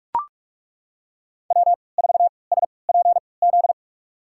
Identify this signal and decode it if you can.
Morse code, heard as a keyed tone, about 42 wpm, spaced as for 25 wpm, 700 Hz. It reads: W4SPZ
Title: USA Call Signs - 3x Character Spacing - 45wpm